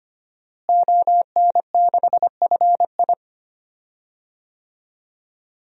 Morse code, heard as a keyed tone, about 25 wpm, 700 Hz. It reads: ON6FI
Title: DXCC-100 Call Signs - 5 Characters - Repeated 3x - 25wpm